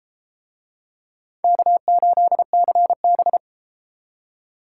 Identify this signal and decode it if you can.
Morse code, heard as a keyed tone, about 31 wpm, 700 Hz. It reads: K8CB